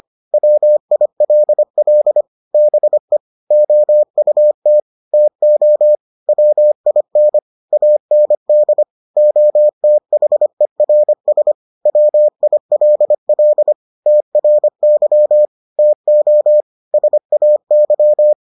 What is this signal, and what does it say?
Morse code, heard as a keyed tone, about 25 wpm, 600 Hz. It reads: WILL BE OUT TO WIN AND OTHERS WILL TRY TO SAY